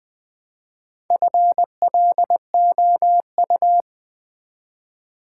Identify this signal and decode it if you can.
Morse code, heard as a keyed tone, about 20 wpm, 700 Hz. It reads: FLOU
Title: Build A Word - 1 Letter At A Time - 3 to 5-Letter Words - 20wpm